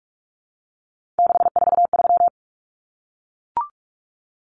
Morse code, heard as a keyed tone, about 44 wpm, 700 Hz. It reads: 643
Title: Numbers - 3-Digit 45wpm